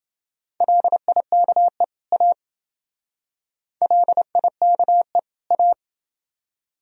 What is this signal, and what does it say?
Morse code, heard as a keyed tone, about 27 wpm, 700 Hz. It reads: LIKE A LIKE A